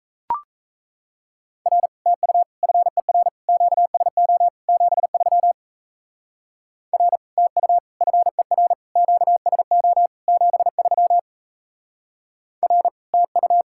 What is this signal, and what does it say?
Morse code, heard as a keyed tone, about 33 wpm, 700 Hz. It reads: R TU FER QSO 73 R TU FER QSO 73 R TU
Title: Speed-Racing Common QSO Phrases Encore - 28wpm